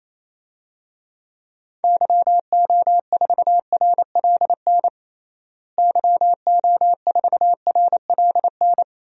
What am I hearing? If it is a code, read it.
Morse code, heard as a keyed tone, about 28 wpm, 700 Hz. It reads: YO4RLN YO4RLN
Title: DXCC-100 Call Signs - 6 Characters - Repeated 2x - 28wpm